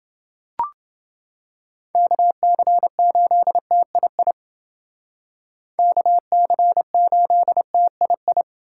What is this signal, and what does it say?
Morse code, heard as a keyed tone, about 29 wpm, 700 Hz. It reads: KC8TII KC8TII